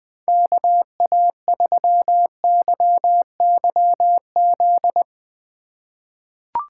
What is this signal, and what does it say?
Morse code, heard as a keyed tone, about 21 wpm, 700 Hz. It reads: KA3YYZ
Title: USA Call Signs - 6 Characters - 2x Character Spacing - 20wpm